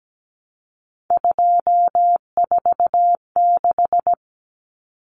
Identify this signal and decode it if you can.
Morse code, heard as a keyed tone, about 17 wpm, 700 Hz. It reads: 246
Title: Numbers - 3-Digit 17wpm